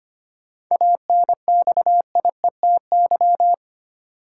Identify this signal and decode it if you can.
Morse code, heard as a keyed tone, about 25 wpm, 700 Hz. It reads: ANXIETY
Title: Sentences from Top 1600 Words - New Word Drill - 25wpm